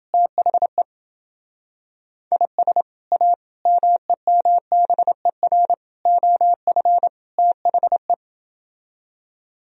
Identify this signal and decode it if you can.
Morse code, heard as a keyed tone, about 28 wpm, 700 Hz. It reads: THE IS A MEMBER OF THE